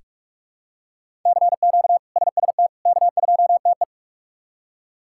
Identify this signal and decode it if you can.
Morse code, heard as a keyed tone, about 46 wpm, 700 Hz. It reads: CQ SST K2TE
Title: K1USN SST QSO Practice - 45wpm